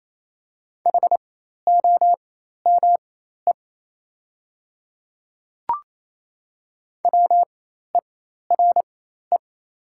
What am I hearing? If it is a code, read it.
Morse code, heard as a keyed tone, about 27 wpm, spaced as for 13 wpm, 690 Hz. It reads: HOME WERE
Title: 4x Character Spacing - Top 200 Words - 28wpm